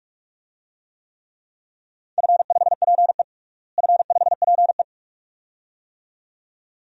Morse code, heard as a keyed tone, about 46 wpm, 700 Hz. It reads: F5PE F5PE
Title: DXCC-100 Call Signs - 4 Characters - Repeated 2x - 45wpm